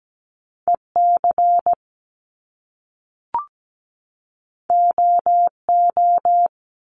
Morse code, heard as a keyed tone, about 17 wpm, 700 Hz. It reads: EC OO
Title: Top Two Letter Combinations in Words 17wpm